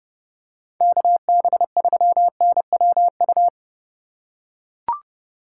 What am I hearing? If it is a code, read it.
Morse code, heard as a keyed tone, about 28 wpm, 700 Hz. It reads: KB3NWU